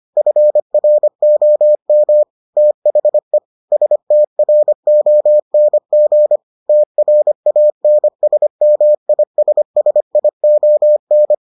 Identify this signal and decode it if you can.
Morse code, heard as a keyed tone, about 25 wpm, 600 Hz. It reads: FROM THE STRONG TRANSMISSION